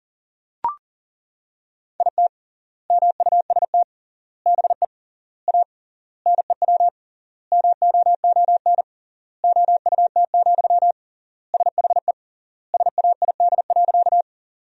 Morse code, heard as a keyed tone, about 40 wpm, 700 Hz. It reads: IT MUST BE A NEW MOON OUT, SHE SAID.
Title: 3x Word-Spacing - Sentences from Top 500 Words - 40wpm